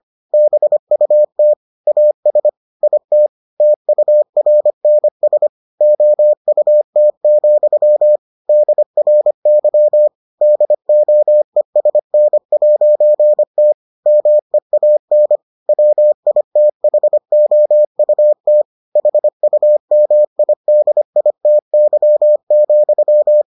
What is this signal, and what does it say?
Morse code, heard as a keyed tone, about 25 wpm, 600 Hz. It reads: BUT AS IT TURNS OUT, DRY DOESN'T MEAN WITHOUT HUMIDITY,